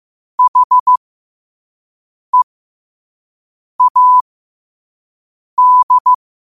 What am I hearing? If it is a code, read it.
Morse code, heard as a keyed tone, about 15 wpm, spaced as for 5 wpm, 1000 Hz. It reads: HEAD
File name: qotdfiles / 5wpm